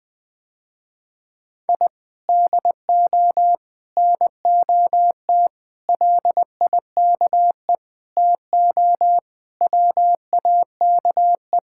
Morse code, heard as a keyed tone, about 20 wpm, 700 Hz. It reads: I DO NOT LIKE TO WAKE
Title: Sentences with 5-Word Phrases - Spoken Beginning - 20wpm